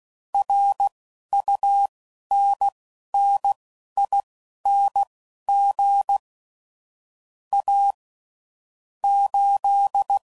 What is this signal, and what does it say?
Morse code, heard as a keyed tone, about 16 wpm, spaced as for 11 wpm, 800 Hz. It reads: RUNNING A 8
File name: CW Code Groups @15WPM / QSOs / WD0FBT Spaced